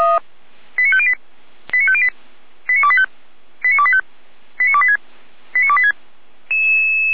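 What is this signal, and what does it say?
Five-tone selective call (ZVEI), heard as quick runs of five tones, tones of about 70 ms. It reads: 89508 89508 89285 89285 89286 89286
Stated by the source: Five-tone ZVEI selective call 89280 received on an SDR.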